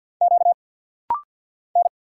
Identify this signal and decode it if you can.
Morse code, heard as a keyed tone, about 50 wpm, 700 Hz. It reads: Q N
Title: Single Letters - Rapid-Fire 50wpm